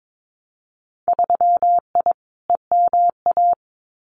Morse code, heard as a keyed tone, about 22 wpm, 700 Hz. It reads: 3I EMA
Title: Winter Field Day Exchanges Encore 22wpm